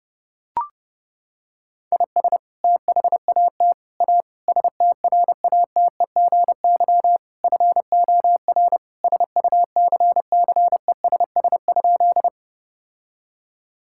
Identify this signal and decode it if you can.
Morse code, heard as a keyed tone, about 30 wpm, 700 Hz. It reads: IS THAT A STRATEGY FOR SUCCESS?
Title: Sentences from Top 1000 Words - Review Part 5 - 30wpm